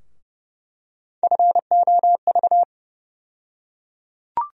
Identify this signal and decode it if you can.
Morse code, heard as a keyed tone, about 32 wpm, 700 Hz. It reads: FOV
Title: USA Call Sign Postfixes - 3 Letters 30wpm